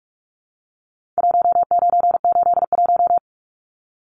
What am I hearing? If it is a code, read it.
Morse code, heard as a keyed tone, about 44 wpm, 700 Hz. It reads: 1981
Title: Numbers - Contest Style - 4-Digit 45wpm